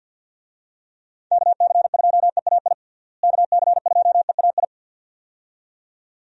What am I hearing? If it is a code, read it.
Morse code, heard as a keyed tone, about 49 wpm, 700 Hz. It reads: KC2EAI KC2EAI
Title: USA Call Signs - 6 Characters - Repeated 2x - 50wpm